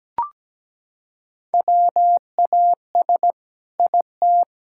Morse code, heard as a keyed tone, about 17 wpm, 700 Hz. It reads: WAS IT
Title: Sentences from Top 1000 Words - Review Part 6 - 17wpm